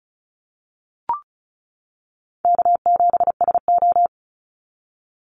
Morse code, heard as a keyed tone, about 34 wpm, 700 Hz. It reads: K7SO
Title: NA SOTA Activator Callsigns 35wpm